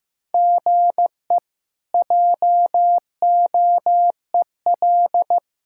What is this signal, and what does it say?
Morse code, heard as a keyed tone, about 15 wpm, 700 Hz. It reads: GE JOEL
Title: K1USN SST QSO Practice - 15wpm